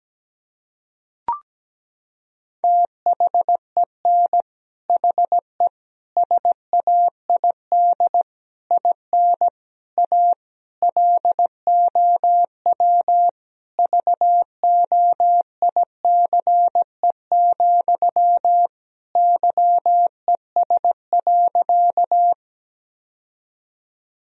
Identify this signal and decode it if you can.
Morse code, heard as a keyed tone, about 17 wpm, 700 Hz. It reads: THEN HE SAID IN A LOW VOICE, YES.
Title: Sentences from Top 1000 Words - Review Part 1 - 17wpm